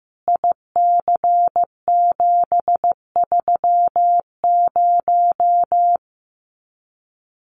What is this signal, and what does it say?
Morse code, heard as a keyed tone, about 15 wpm, 700 Hz. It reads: IC730
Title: Common QSO Phrases 15wpm